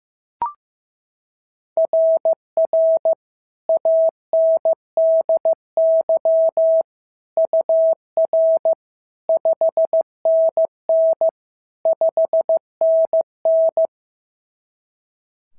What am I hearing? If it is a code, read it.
Morse code, heard as a keyed tone, about 14 wpm, 650 Hz. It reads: RR ANDY UR 5NN 5NN
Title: SOTA QSO Practice 15wpm